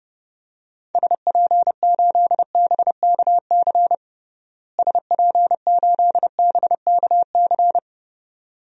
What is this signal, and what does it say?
Morse code, heard as a keyed tone, about 30 wpm, 700 Hz. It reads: SP8BKC SP8BKC
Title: DXCC-100 Call Signs - 6 Characters - Repeated 2x - 30wpm